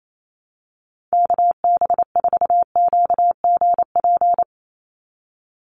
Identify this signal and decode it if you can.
Morse code, heard as a keyed tone, about 28 wpm, 700 Hz. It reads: KB4QGP
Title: USA Call Signs 28wpm